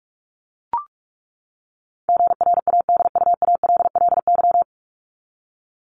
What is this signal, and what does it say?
Morse code, heard as a keyed tone, about 44 wpm, 700 Hz. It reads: GRADUALLY
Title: LY Suffix - Top 100 Words with Base Words 45wpm